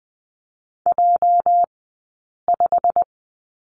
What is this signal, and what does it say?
Morse code, heard as a keyed tone, about 20 wpm, spaced as for 8 wpm, 700 Hz. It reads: J5
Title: 204 Lesson 33d - Letter j - Two Characters 20wpm